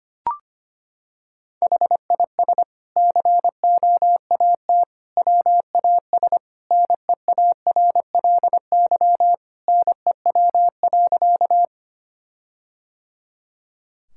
Morse code, heard as a keyed tone, about 25 wpm, 700 Hz. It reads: HIS COAT WAS NEARLY NEW.
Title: Sentences from Top 1600 Words 25wpm